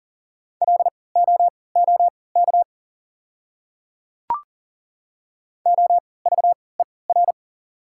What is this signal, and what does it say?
Morse code, heard as a keyed tone, about 40 wpm, spaced as for 23 wpm, 690 Hz. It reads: LOOK OVER
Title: 3x Character Spacing - Top 100 Words - 40wpm